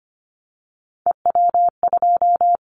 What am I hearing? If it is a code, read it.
Morse code, heard as a keyed tone, about 25 wpm, 700 Hz. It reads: EW2
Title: Callsign Prefixes for Top 100 DXCC - 25wpm